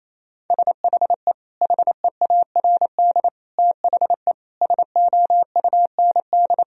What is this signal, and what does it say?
Morse code, heard as a keyed tone, about 28 wpm, 700 Hz. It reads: SHE HEARD THE SOUND
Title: Sentences from Top 1000 Words - Review Part 1 - 28wpm